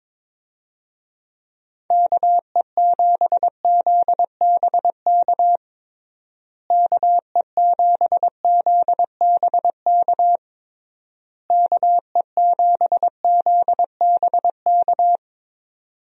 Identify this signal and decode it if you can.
Morse code, heard as a keyed tone, about 21 wpm, 700 Hz. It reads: KE7ZBK KE7ZBK KE7ZBK
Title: USA Call Signs - 6 Characters - Repeated 3x - 22wpm